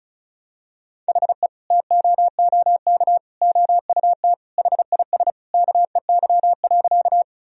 Morse code, heard as a keyed tone, about 34 wpm, 700 Hz. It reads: HE TOOK OUT HIS KEY.
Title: Sentences from Top 1000 Words - Review Part 4 - 35wpm